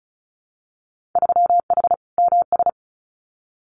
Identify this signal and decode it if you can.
Morse code, heard as a keyed tone, about 36 wpm, 700 Hz. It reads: 3H MS
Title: Winter Field Day Exchanges Encore 35wpm